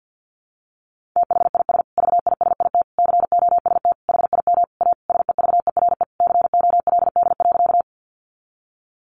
Morse code, heard as a keyed tone, about 50 wpm, 700 Hz. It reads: THIS VISIT COST HIM A SEVERE COLD.
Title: Sentences from Top 1400 Words 50wpm